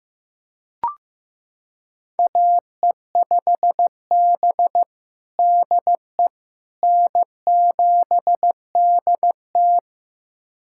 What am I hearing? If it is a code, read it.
Morse code, heard as a keyed tone, about 15 wpm, 700 Hz. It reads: AE5B DE N7DT
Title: USA Callsign DE Callsign - 4 Characters - 15wpm